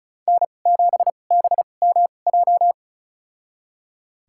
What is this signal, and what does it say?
Morse code, heard as a keyed tone, about 36 wpm, spaced as for 25 wpm, 700 Hz. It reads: N7BMJ
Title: USA Call Signs - 5 Characters - 2x Character Spacing - 35wpm